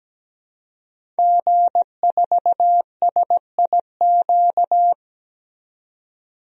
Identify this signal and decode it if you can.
Morse code, heard as a keyed tone, about 17 wpm, 700 Hz. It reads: G4SIQ
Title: DXCC-100 Call Signs - 5 Characters - 17wpm